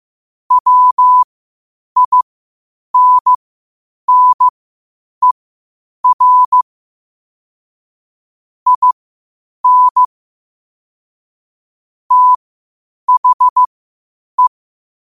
Morse code, heard as a keyed tone, about 15 wpm, spaced as for 8 wpm, 1000 Hz. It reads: WINNER IN THE
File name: qotdfiles / 8wpm